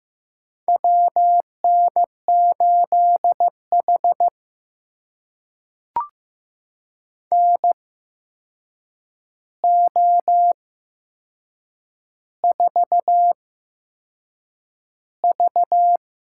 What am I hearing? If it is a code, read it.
Morse code, heard as a keyed tone, about 15 wpm, spaced as for 6 wpm, 700 Hz. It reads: WN8H NO4V
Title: USA Call Signs - 4 Characters - 8x Character Spacing - 15wpm